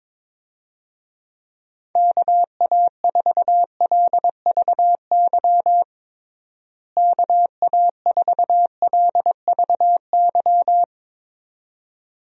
Morse code, heard as a keyed tone, about 22 wpm, 700 Hz. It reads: KA4LVY KA4LVY
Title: USA Call Signs - 6 Characters - Repeated 2x - 22wpm